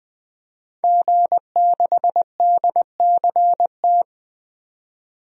Morse code, heard as a keyed tone, about 20 wpm, 700 Hz. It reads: G6DCT
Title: DXCC-100 Call Signs - 5 Characters - 20wpm